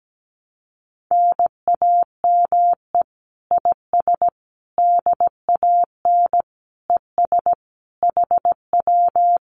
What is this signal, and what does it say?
Morse code, heard as a keyed tone, about 17 wpm, 700 Hz. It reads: NAME IS DAN ES HW
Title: Common QSO Phrases Encore 17wpm